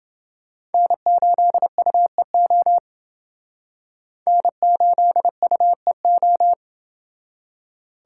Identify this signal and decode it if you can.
Morse code, heard as a keyed tone, about 28 wpm, 700 Hz. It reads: N8UEO N8UEO